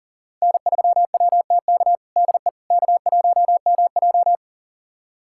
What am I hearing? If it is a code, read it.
Morse code, heard as a keyed tone, about 40 wpm, 700 Hz. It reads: N3WTK DE K1MJ
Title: USA Callsign DE Callsign 40wpm